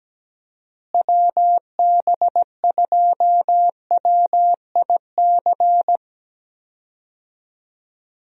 Morse code, heard as a keyed tone, about 17 wpm, 700 Hz. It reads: WB2WIC
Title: USA Call Signs - 6 Characters - Repeated 3x - 17wpm